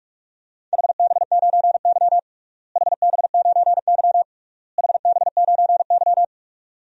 Morse code, heard as a keyed tone, about 46 wpm, 700 Hz. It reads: HB9Y HB9Y HB9Y
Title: DXCC-100 Call Signs - 4 Characters - Repeated 3x - 45wpm